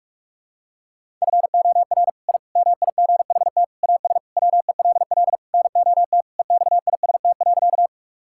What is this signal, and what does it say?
Morse code, heard as a keyed tone, about 46 wpm, 700 Hz. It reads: FOR I MIGHT AS WELL NOT EXIST.
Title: Sentences from Top 1000 Words - Review Part 5 - 45wpm